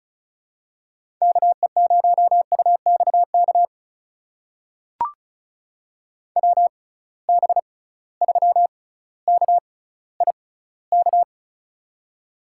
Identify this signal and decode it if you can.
Morse code, heard as a keyed tone, about 36 wpm, spaced as for 18 wpm, 700 Hz. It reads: KE0UXK WB3KIK
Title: USA Call Signs - 6 Characters - 6x Character Spacing - 35wpm